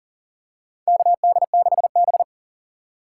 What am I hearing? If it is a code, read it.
Morse code, heard as a keyed tone, about 40 wpm, 700 Hz. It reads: KD6B
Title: USA Call Signs - 4 Characters - 6x Character Spacing - 40wpm